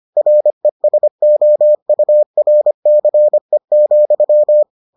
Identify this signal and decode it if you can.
Morse code, heard as a keyed tone, about 25 wpm, 600 Hz. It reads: RESOURCE,